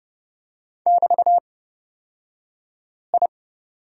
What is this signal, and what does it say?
Morse code, heard as a keyed tone, about 29 wpm, spaced as for 5 wpm, 700 Hz. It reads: <BT>I
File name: Lesson 38d - Letter q - Two Characters 30wpm